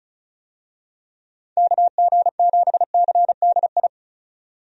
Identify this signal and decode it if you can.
Morse code, heard as a keyed tone, about 34 wpm, 700 Hz. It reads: KG7CDI
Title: USA Call Signs - 6 Characters - 35wpm